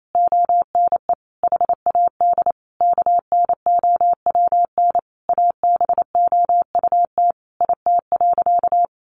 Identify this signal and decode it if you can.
Morse code, heard as a keyed tone, about 28 wpm, 700 Hz. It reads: ONE HAD KNOWN ABOUT IT.
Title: Sentences from Top 1200 Words 28wpm